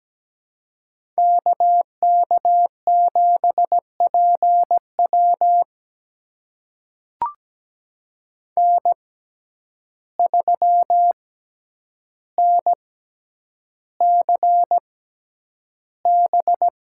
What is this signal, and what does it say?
Morse code, heard as a keyed tone, about 17 wpm, spaced as for 9 wpm, 700 Hz. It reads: KK7PW N3NCB
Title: USA Call Signs - 5 Characters - 6x Character Spacing - 17wpm